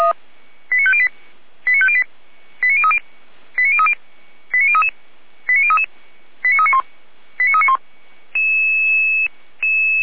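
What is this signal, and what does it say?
Five-tone selective call (ZVEI), heard as quick runs of five tones, tones of about 70 ms. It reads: 89508 89508 89030 89030 89033 89033 89391 89391
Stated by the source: ZVEI five-tone selective call received by SDR.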